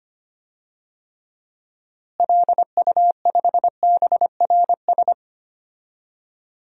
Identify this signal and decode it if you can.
Morse code, heard as a keyed tone, about 25 wpm, 700 Hz. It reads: LU5BRS
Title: DXCC-100 Call Signs - 6 Characters - Repeated 3x - 25wpm